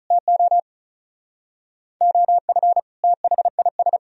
Morse code, heard as a keyed tone, about 43 wpm, 700 Hz. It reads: TO OF THIS
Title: Speed-Racing - Sets of 5 Words 35wpm